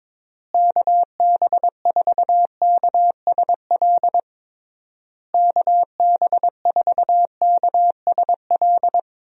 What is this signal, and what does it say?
Morse code, heard as a keyed tone, about 22 wpm, 700 Hz. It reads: KB4KSL KB4KSL
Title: USA Call Signs - 6 Characters - Repeated 3x - 22wpm